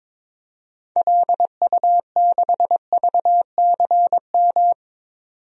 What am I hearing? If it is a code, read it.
Morse code, heard as a keyed tone, about 21 wpm, 700 Hz. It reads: LU6VCM
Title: DXCC-100 Call Signs - 6 Characters - 22wpm